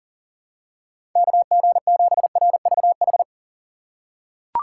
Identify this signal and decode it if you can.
Morse code, heard as a keyed tone, about 40 wpm, 700 Hz. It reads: KG7RVH
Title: USA Call Signs - 6 Characters - 6x Character Spacing - 40wpm